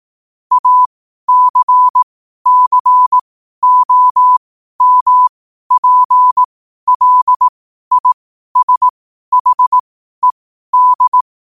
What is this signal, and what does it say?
Morse code, heard as a keyed tone, about 18 wpm, spaced as for 12 wpm, 1000 Hz. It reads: ACCOMPLISHED
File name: qotdfiles / 12wpm